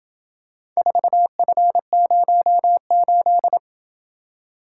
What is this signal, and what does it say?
Morse code, heard as a keyed tone, about 27 wpm, 700 Hz. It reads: 4F08